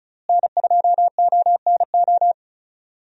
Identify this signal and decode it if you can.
Morse code, heard as a keyed tone, about 36 wpm, 700 Hz. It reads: N2ONO
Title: USA Call Signs - 5 Characters - Repeated 3x - 35wpm